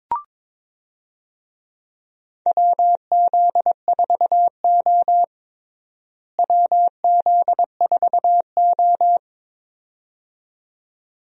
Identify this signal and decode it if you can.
Morse code, heard as a keyed tone, about 22 wpm, 700 Hz. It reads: WZ4O WZ4O
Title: USA Call Signs - 4 Characters - Repeated 2x - 22wpm